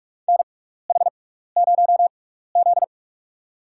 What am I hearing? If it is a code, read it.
Morse code, heard as a keyed tone, about 42 wpm, spaced as for 15 wpm, 700 Hz. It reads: NH0Z
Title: USA Call Signs - 4 Characters - 6x Character Spacing - 45wpm